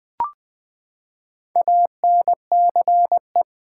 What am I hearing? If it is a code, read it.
Morse code, heard as a keyed tone, about 20 wpm, 700 Hz. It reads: ANCE
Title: Build A Word - 4 Letters At A Time - 8-Letter Words - 20wpm